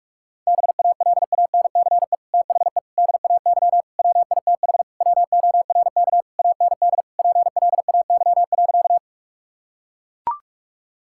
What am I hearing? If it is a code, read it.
Morse code, heard as a keyed tone, about 41 wpm, 700 Hz. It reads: BALANCE THE DAY WITH WORK AND PLAY.
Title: Sentences from Top 1000 Words - Review Part 6 - 45wpm